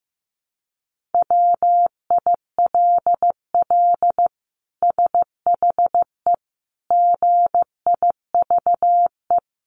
Morse code, heard as a keyed tone, about 15 wpm, 700 Hz. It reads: WILL SHE GIVE
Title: Sentences from Top 1400 Words 15wpm